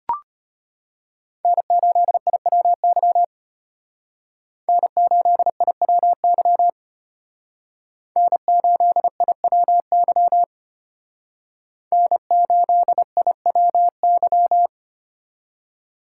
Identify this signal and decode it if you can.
Morse code, heard as a keyed tone, about 33 wpm, 700 Hz. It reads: N8IWY N8IWY N8IWY N8IWY